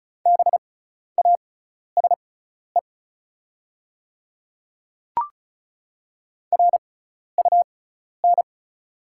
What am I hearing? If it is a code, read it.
Morse code, heard as a keyed tone, about 36 wpm, spaced as for 12 wpm, 700 Hz. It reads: BASE RUN
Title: 6x Character Spacing - Top 300 Words - 35wpm